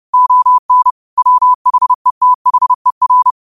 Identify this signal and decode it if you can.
Morse code, heard as a keyed tone, about 30 wpm, 1000 Hz. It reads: ON WHETHER